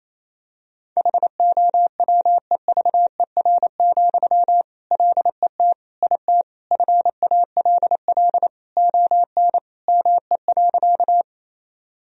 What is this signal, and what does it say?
Morse code, heard as a keyed tone, about 27 wpm, 700 Hz. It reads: HOWEVER, LET IT FALL ON ME.
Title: Sentences from Top 1000 Words - Review Part 2 - 28wpm